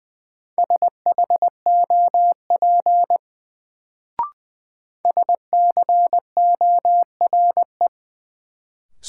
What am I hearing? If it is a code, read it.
Morse code, heard as a keyed tone, about 20 wpm, 700 Hz. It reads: SHOP SCORE
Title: Sentences from Top 1600 Words - New Word Drill - 20wpm